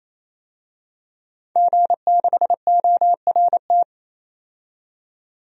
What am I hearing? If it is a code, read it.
Morse code, heard as a keyed tone, about 28 wpm, 700 Hz. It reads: G6ORT
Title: DXCC-100 Call Signs - 5 Characters - 28wpm